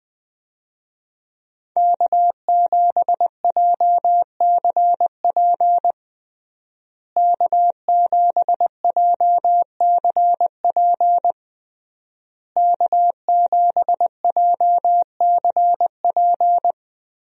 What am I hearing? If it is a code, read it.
Morse code, heard as a keyed tone, about 20 wpm, 700 Hz. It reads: K7JCP K7JCP K7JCP
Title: USA Call Signs - 5 Characters - Repeated 3x - 20wpm